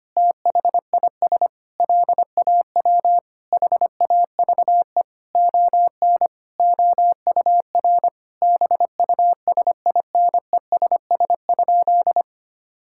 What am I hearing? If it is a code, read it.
Morse code, heard as a keyed tone, about 25 wpm, 700 Hz. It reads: THIS LAW HAVE ON OUR BUSINESS?
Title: Sentences from Top 1000 Words - Review Part 6 - 25wpm